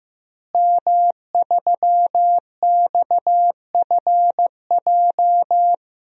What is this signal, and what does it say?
Morse code, heard as a keyed tone, about 15 wpm, 700 Hz. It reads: M3XFJ
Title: DXCC-100 Call Signs - 5 Characters - Repeated 2x - 15wpm